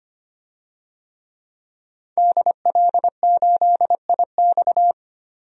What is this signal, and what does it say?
Morse code, heard as a keyed tone, about 25 wpm, 700 Hz. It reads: DL8IX
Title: DXCC-100 Call Signs - 5 Characters - 25wpm